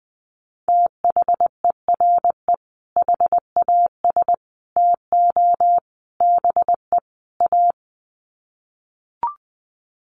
Morse code, heard as a keyed tone, about 20 wpm, 700 Hz. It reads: THERE HAS TO BE A
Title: Sets of 5 Words Encore 20wpm